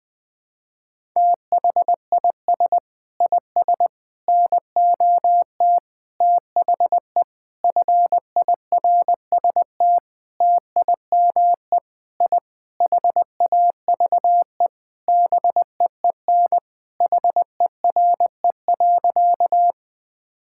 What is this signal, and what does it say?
Morse code, heard as a keyed tone, about 21 wpm, 700 Hz. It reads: THIS IS NOT THE FIRST TIME I HAVE BEEN HERE.
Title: Sentences with 6-Word Phrases - 20wpm